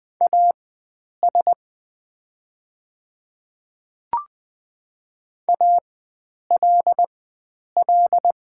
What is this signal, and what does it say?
Morse code, heard as a keyed tone, about 20 wpm, spaced as for 9 wpm, 700 Hz. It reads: AS ALL
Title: 4x Character Spacing - Top 100 Words - 20wpm